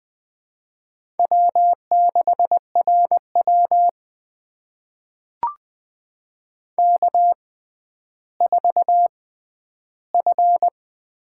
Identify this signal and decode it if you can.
Morse code, heard as a keyed tone, about 20 wpm, spaced as for 11 wpm, 700 Hz. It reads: W6RW K4F
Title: USA Call Signs - 4 Characters - 6x Character Spacing - 20wpm